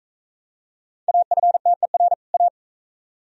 Morse code, heard as a keyed tone, about 40 wpm, 700 Hz. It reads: AFTER A